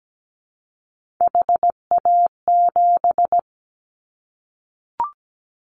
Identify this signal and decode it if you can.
Morse code, heard as a keyed tone, about 17 wpm, 700 Hz. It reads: HA7
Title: Callsign Prefixes for Top 25 DXCC - 17wpm